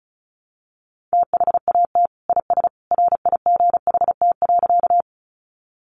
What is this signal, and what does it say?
Morse code, heard as a keyed tone, about 34 wpm, 700 Hz. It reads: THAT IS RIGHT.
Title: Sentences from Top 200 Words 35wpm